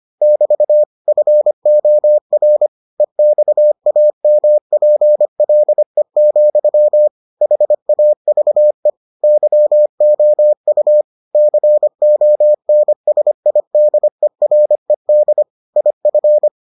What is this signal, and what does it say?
Morse code, heard as a keyed tone, about 25 wpm, 600 Hz. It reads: <BT> FOR EXAMPLE, HAVE YOU CONSIDERED IF